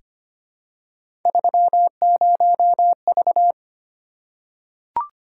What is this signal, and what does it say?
Morse code, heard as a keyed tone, about 25 wpm, 700 Hz. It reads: 30V
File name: Three Letter-Number Combinations - 25wpm